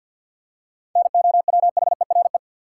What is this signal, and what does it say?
Morse code, heard as a keyed tone, about 48 wpm, 700 Hz. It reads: NOWHERE
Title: Seven Letter Words - 50wpm